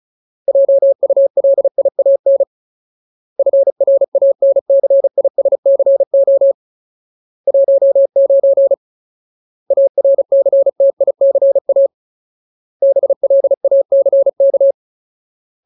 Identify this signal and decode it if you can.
Morse code, heard as a keyed tone, about 35 wpm, 550 Hz. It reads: JULIAN FRANCISCO 19 ARCTICA BLACK
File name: Ultimate Word List / 35WPM Ultimate